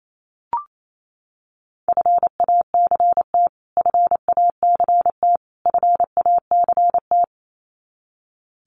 Morse code, heard as a keyed tone, about 28 wpm, 700 Hz. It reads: FACT FACT FACT
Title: Top 500 Words Repeated 28wpm